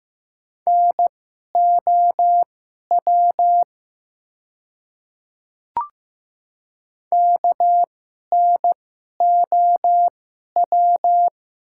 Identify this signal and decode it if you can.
Morse code, heard as a keyed tone, about 15 wpm, spaced as for 11 wpm, 700 Hz. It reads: NOW KNOW
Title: 2x Character Spacing - Top 100 Words - 15wpm